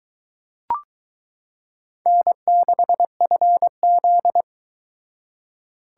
Morse code, heard as a keyed tone, about 22 wpm, 700 Hz. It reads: N6FZ